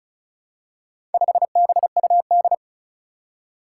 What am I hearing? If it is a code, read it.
Morse code, heard as a keyed tone, about 36 wpm, 700 Hz. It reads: 5BUD
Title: Four Challenging Letter-Number Combinations 35wpm